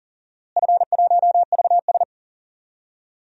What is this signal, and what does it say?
Morse code, heard as a keyed tone, about 39 wpm, 700 Hz. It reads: F1VS
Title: DXCC-100 Call Signs - 4 Characters - Repeated 2x - 40wpm